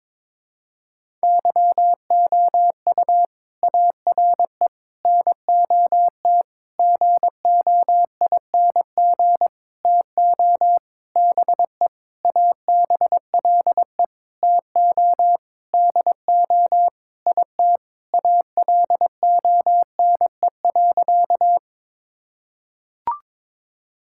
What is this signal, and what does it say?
Morse code, heard as a keyed tone, about 22 wpm, 700 Hz. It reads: YOU ARE NOT GOING TO BE ABLE TO DO IT ALONE.
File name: Sentences with 7-Word Phrases - 22wpm